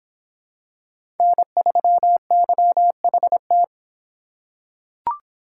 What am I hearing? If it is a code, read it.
Morse code, heard as a keyed tone, about 25 wpm, 700 Hz. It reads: N3YHT